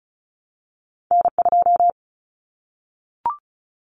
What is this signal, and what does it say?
Morse code, heard as a keyed tone, about 36 wpm, 700 Hz. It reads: N2
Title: Common Cut Numbers - 2-Digit 35wpm